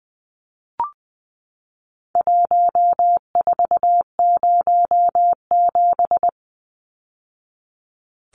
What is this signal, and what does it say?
Morse code, heard as a keyed tone, about 20 wpm, 700 Hz. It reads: 1407
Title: Numbers - Contest Style - 4-Digit 20wpm